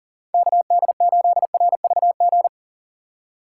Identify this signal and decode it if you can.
Morse code, heard as a keyed tone, about 40 wpm, 700 Hz. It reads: KD8RVG
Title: USA Call Signs - 6 Characters - 6x Character Spacing - 40wpm